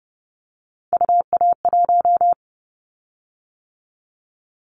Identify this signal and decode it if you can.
Morse code, heard as a keyed tone, about 30 wpm, 700 Hz. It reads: UA1
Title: Callsign Prefixes for Top 100 DXCC - 30wpm